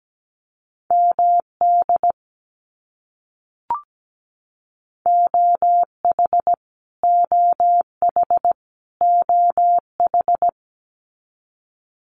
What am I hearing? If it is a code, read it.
Morse code, heard as a keyed tone, about 17 wpm, 700 Hz. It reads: MD OH OH OH
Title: US States Repeated 3x 17wpm